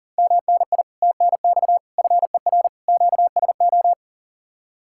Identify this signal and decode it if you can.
Morse code, heard as a keyed tone, about 50 wpm, 700 Hz. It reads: MNI TNX FER QSO
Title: Speed-Racing Common QSO Phrases - 40wpm